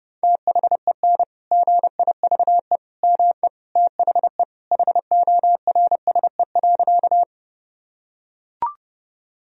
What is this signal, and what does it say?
Morse code, heard as a keyed tone, about 30 wpm, 700 Hz. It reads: THEN GIVE ME THE HORSE.
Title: Sentences from Top 1000 Words - Review Part 1 - 30wpm